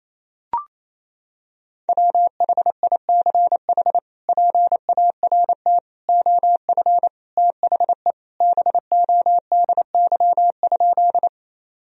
Morse code, heard as a keyed tone, about 27 wpm, 700 Hz. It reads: WHICH PART OF THE BODY?
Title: Sentences from Top 1000 Words - Review Part 2 - 28wpm